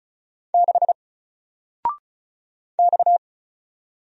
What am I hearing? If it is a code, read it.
Morse code, heard as a keyed tone, about 36 wpm, 700 Hz. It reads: 6 X
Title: Single Character (Letter:Number:Punctuation:Prosign) 35wpm